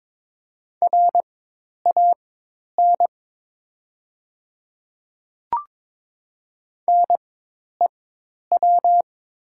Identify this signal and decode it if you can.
Morse code, heard as a keyed tone, about 22 wpm, spaced as for 10 wpm, 700 Hz. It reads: RAN NEW
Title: 4x Character Spacing - Top 500 Words - 22wpm